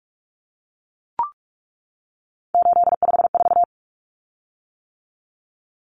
Morse code, heard as a keyed tone, about 44 wpm, 700 Hz. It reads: 854
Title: Common Cut Numbers - 3-Digit 45wpm